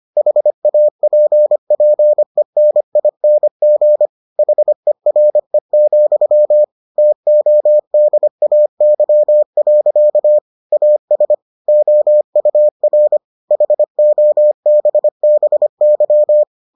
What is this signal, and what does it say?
Morse code, heard as a keyed tone, about 25 wpm, 600 Hz. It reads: HAPPENING HERE, TODAY. AS OUR HOBBY